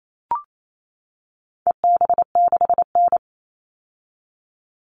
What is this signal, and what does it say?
Morse code, heard as a keyed tone, about 28 wpm, 700 Hz. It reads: EB6N
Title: All Cut Numbers - 4-Digit 28wpm